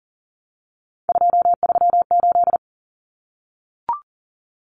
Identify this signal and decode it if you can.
Morse code, heard as a keyed tone, about 40 wpm, 700 Hz. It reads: 238
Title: Common Cut Numbers - 3-Digit 40wpm